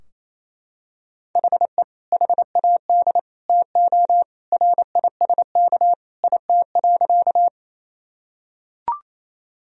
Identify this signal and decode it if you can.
Morse code, heard as a keyed tone, about 28 wpm, 700 Hz. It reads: HE HAD TO RISK IT.
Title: Sentences from Top 1000 Words - Review Part 4 - 28wpm